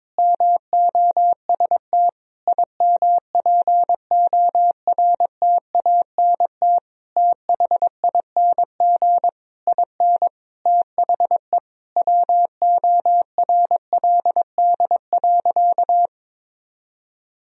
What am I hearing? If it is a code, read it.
Morse code, heard as a keyed tone, about 22 wpm, 700 Hz. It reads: MOST IMPORTANT THING IN THE WORLD.
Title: Sentences from Top 1000 Words - Review Part 1 - 22wpm